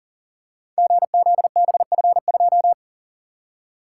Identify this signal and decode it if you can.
Morse code, heard as a keyed tone, about 40 wpm, 690 Hz. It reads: GZBF2
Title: Code Groups - 5 Letter:Numbers - 40wpm